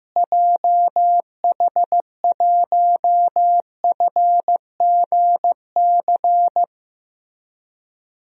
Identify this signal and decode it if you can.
Morse code, heard as a keyed tone, about 15 wpm, 700 Hz. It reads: JH1FGC
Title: DXCC-100 Call Signs - 6 Characters - Repeated 3x - 15wpm